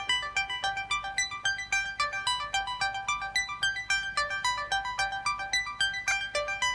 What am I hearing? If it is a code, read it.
Emergency Alert System (SAME) tones (header bursts), heard as harsh buzzing bursts. It is heard from 1.1 to 6.3 s.